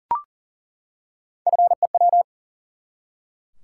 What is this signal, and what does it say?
Morse code, heard as a keyed tone, about 38 wpm, 700 Hz. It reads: FEW
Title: Three Letter Words - 40wpm